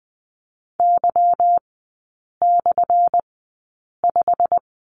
Morse code, heard as a keyed tone, about 20 wpm, spaced as for 8 wpm, 700 Hz. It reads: Y/5
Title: 178 Lesson 29e - Slash - Three Characters 20wpm